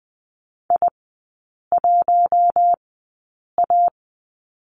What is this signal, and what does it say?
Morse code, heard as a keyed tone, about 20 wpm, spaced as for 8 wpm, 700 Hz. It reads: I1A
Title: 053 Lesson 10e - Letter r - Three Characters 20wpm